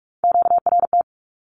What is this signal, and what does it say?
Morse code, heard as a keyed tone, about 44 wpm, 700 Hz. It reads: QRT
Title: Q-Codes 45wpm